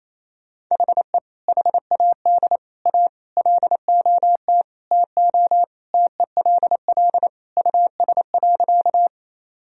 Morse code, heard as a keyed tone, about 28 wpm, 700 Hz. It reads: HE HAD A LOT TO TELL US.
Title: Sentences from Top 1000 Words - Review Part 3 - 28wpm